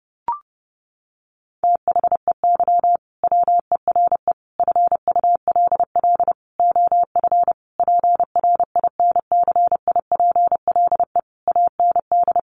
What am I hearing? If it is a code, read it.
Morse code, heard as a keyed tone, about 30 wpm, 700 Hz. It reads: THEY WERE FULL OF PRINCIPLE AND
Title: Sentences from Top 1400 Words 30wpm